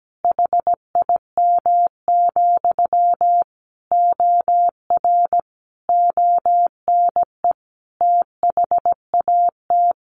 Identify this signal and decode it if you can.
Morse code, heard as a keyed tone, about 17 wpm, 700 Hz. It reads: HIM, OR ONE THAT
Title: A Sound Of Thunder 17wpm